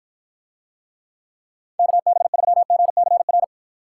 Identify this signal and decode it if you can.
Morse code, heard as a keyed tone, about 54 wpm, 700 Hz. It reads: KB3GCR